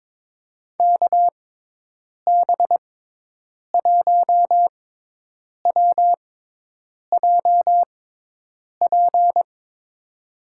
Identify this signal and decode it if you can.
Morse code, heard as a keyed tone, about 22 wpm, spaced as for 8 wpm, 700 Hz. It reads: KB1WJP
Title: USA Call Signs - 6 Characters - 6x Character Spacing - 22wpm